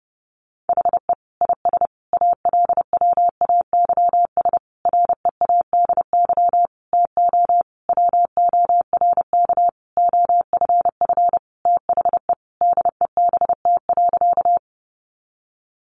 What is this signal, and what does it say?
Morse code, heard as a keyed tone, about 30 wpm, 700 Hz. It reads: HE IS ALWAYS READY TO WORK OFF THE DEBT.
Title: Sentences from Top 1300 Words 30wpm